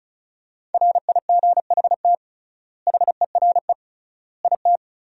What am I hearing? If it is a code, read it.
Morse code, heard as a keyed tone, about 36 wpm, 700 Hz. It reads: RIGHT HERE IT
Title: 3x Word-Spacing - Sentences from Top 300 Words - 35wpm